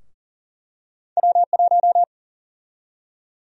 Morse code, heard as a keyed tone, about 40 wpm, 700 Hz. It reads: W1
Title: USA Call Sign Prefixes - Repeated 2x - 40wpm